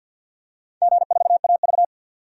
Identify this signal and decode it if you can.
Morse code, heard as a keyed tone, about 52 wpm, 700 Hz. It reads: G4AV
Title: DXCC-100 Call Signs - 4 Characters - 50wpm